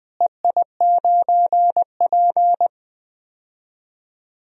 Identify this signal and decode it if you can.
Morse code, heard as a keyed tone, about 20 wpm, 700 Hz. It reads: EI9P
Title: DXCC-100 Call Signs - 4 Characters - Repeated 3x - 20wpm